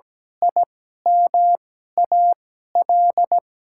Morse code, heard as a keyed tone, about 17 wpm, spaced as for 12 wpm, 700 Hz. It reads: IMAL
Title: 2x Character Spacing - Top 200 Words - 17wpm